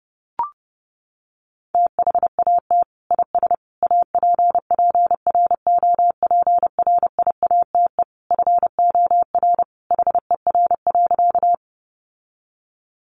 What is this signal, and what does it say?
Morse code, heard as a keyed tone, about 30 wpm, 700 Hz. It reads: THAT IS APPROPRIATE FOR HER.
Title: Sentences from Top 900 Words 30wpm